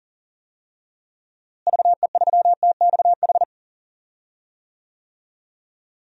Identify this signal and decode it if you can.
Morse code, heard as a keyed tone, about 37 wpm, 700 Hz. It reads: VE3TXH
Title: DXCC-100 Call Signs - 6 Characters - Repeated 3x - 40wpm